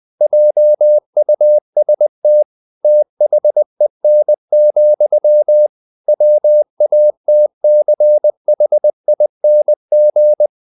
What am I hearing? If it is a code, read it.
Morse code, heard as a keyed tone, about 20 wpm, 600 Hz. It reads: JUST THEN, WATCHING
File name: the old man and the sea / the old man and the sea part 1 20wpm